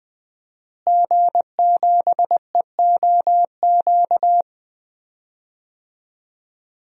Morse code, heard as a keyed tone, about 20 wpm, 700 Hz. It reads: G7EOQ
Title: DXCC-100 Call Signs - 5 Characters - Repeated 2x - 20wpm